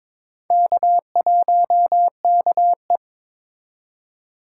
Morse code, heard as a keyed tone, about 22 wpm, 700 Hz. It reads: K1KE
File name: practice / 22wpm / Four Letter-Number Combinations - 22wpm